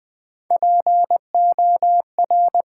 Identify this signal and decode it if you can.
Morse code, heard as a keyed tone, about 20 wpm, 700 Hz. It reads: POR
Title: Build A Word - 3 Letters At A Time - 9-Letter Words - 20wpm